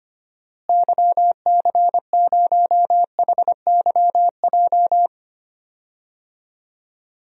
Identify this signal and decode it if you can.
Morse code, heard as a keyed tone, about 25 wpm, 700 Hz. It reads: YC0HYJ
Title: DXCC-100 Call Signs - 6 Characters - Repeated 2x - 25wpm